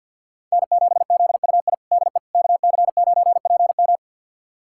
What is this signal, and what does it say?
Morse code, heard as a keyed tone, about 46 wpm, 700 Hz. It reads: N7ZUI DE KC9PM
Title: USA Callsign DE Callsign - 5 Characters - 50wpm